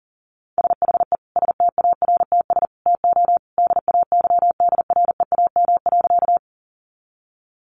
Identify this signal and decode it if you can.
Morse code, heard as a keyed tone, about 40 wpm, 700 Hz. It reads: SHE STARTS TO DAYDREAM.
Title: Easy Phrases 40wpm